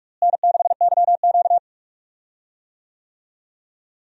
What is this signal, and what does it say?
Morse code, heard as a keyed tone, about 44 wpm, 700 Hz. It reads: N6YQ
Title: USA Call Signs - 4 Characters - Repeated 3x - 45wpm